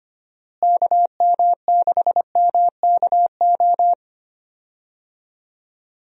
Morse code, heard as a keyed tone, about 25 wpm, 700 Hz. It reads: KM6MKO
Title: USA Call Signs - 6 Characters - Repeated 3x - 25wpm